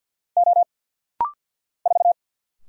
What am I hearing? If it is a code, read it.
Morse code, heard as a keyed tone, about 50 wpm, 700 Hz. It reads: O 4
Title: Single Letter-Number Alt - Rapid-Fire 50wpm